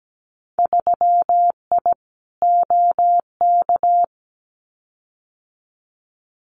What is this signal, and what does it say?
Morse code, heard as a keyed tone, about 17 wpm, 700 Hz. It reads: 3I OK
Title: Winter Field Day Exchanges Encore 17wpm